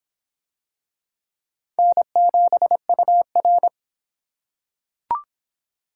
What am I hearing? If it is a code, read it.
Morse code, heard as a keyed tone, about 23 wpm, 700 Hz. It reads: N7UR